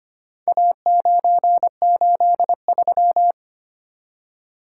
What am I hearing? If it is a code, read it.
Morse code, heard as a keyed tone, about 25 wpm, 700 Hz. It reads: A983
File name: practice / 25wpm / Four Letter-Number Combinations - 25wpm